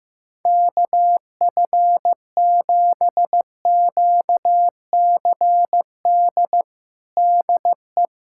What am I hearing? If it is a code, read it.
Morse code, heard as a keyed tone, about 16 wpm, 700 Hz. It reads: KF7QCD DE
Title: USA Callsign DE Callsign 15wpm